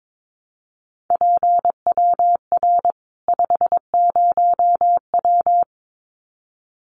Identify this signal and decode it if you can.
Morse code, heard as a keyed tone, about 22 wpm, 700 Hz. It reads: PWR 50W
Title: Common QSO Phrases 22wpm